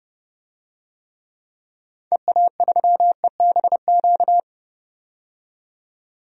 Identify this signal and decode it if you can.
Morse code, heard as a keyed tone, about 30 wpm, 700 Hz. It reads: EA3EBQ
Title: DXCC-100 Call Signs - 6 Characters - 30wpm